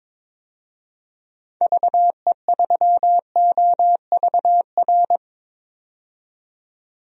Morse code, heard as a keyed tone, about 22 wpm, 700 Hz. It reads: VE3OVR
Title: DXCC-100 Call Signs - 6 Characters - 22wpm